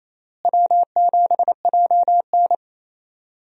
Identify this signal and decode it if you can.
Morse code, heard as a keyed tone, about 27 wpm, 700 Hz. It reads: W7JN
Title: USA Call Signs - 4 Characters - Repeated 3x - 28wpm